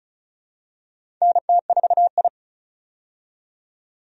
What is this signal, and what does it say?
Morse code, heard as a keyed tone, about 34 wpm, 700 Hz. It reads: NT4I
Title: USA Call Signs - 4 Characters - 35wpm